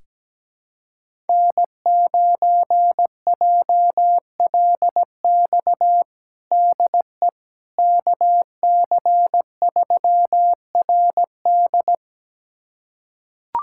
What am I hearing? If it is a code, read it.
Morse code, heard as a keyed tone, about 17 wpm, 700 Hz. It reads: N9JLX DE KC3RD
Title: USA Callsign DE Callsign 17wpm